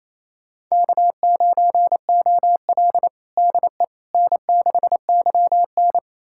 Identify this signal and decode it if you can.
Morse code, heard as a keyed tone, about 28 wpm, 700 Hz. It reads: K9OL DE N6YN
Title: USA Callsign DE Callsign - 4 Characters - 28wpm